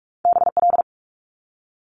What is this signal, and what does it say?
Morse code, heard as a keyed tone, about 46 wpm, 700 Hz. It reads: BL
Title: Multi-letter Phonemes 45wpm